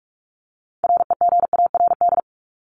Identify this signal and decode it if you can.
Morse code, heard as a keyed tone, about 44 wpm, 700 Hz. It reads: REGARD
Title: ED Suffix - Top 100 Words with Base Words 45wpm